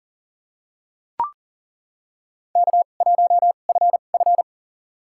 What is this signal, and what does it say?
Morse code, heard as a keyed tone, about 40 wpm, spaced as for 29 wpm, 700 Hz. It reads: K1FF
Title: USA Call Signs - 4 Characters - 2x Character Spacing - 40wpm